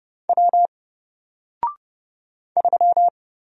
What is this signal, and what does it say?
Morse code, heard as a keyed tone, about 29 wpm, 700 Hz. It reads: W 3
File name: Single-Letter-Number-30wpm